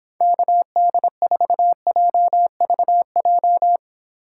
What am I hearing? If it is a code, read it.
Morse code, heard as a keyed tone, about 26 wpm, 700 Hz. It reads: KD4JVJ